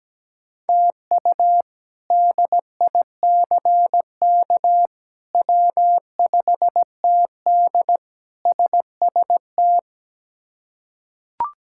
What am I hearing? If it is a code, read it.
Morse code, heard as a keyed tone, about 17 wpm, 700 Hz. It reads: TU DICK W5TD SST
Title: K1USN SST QSO Practice - 17wpm